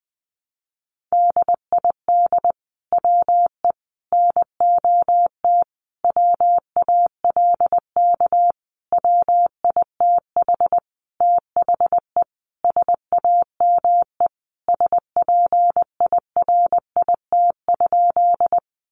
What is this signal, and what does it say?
Morse code, heard as a keyed tone, about 19 wpm, 700 Hz. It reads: DID WE NOT WALK WITH THE SAME SPIRIT?
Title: Sentences from Top 1400 Words 20wpm